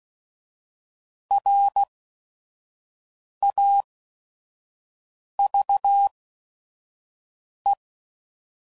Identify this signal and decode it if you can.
Morse code, heard as a keyed tone, about 16 wpm, spaced as for 5 wpm, 800 Hz. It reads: RAVE
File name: Prefixes / By First Letter / Rs